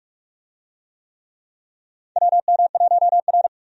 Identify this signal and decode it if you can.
Morse code, heard as a keyed tone, about 46 wpm, 700 Hz. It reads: WM1R
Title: USA Call Signs - 4 Characters - 45wpm